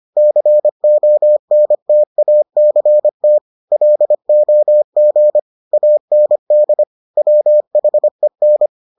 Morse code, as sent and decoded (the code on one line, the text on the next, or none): -.-. --- -. - .- -.-. - / .-.. --- --. / .- -. -.. / .-- .... -.
CONTACT LOG AND WHEN